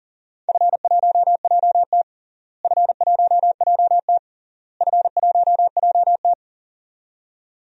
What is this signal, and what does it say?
Morse code, heard as a keyed tone, about 40 wpm, 700 Hz. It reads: F1JT F1JT F1JT
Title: DXCC-100 Call Signs - 4 Characters - Repeated 3x - 40wpm